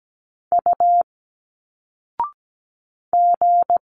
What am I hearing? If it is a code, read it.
Morse code, heard as a keyed tone, about 17 wpm, 700 Hz. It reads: U G
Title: Single Letters 17wpm